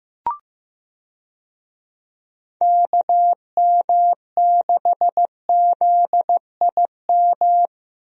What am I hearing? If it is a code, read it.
Morse code, heard as a keyed tone, about 15 wpm, 700 Hz. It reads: KM6ZIM
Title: USA Call Signs - 6 Characters - Repeated 2x - 15wpm